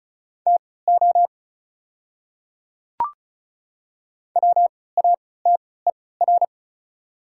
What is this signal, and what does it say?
Morse code, heard as a keyed tone, about 33 wpm, spaced as for 19 wpm, 690 Hz. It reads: TO WATER
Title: 3x Character Spacing - Top 100 Words - 35wpm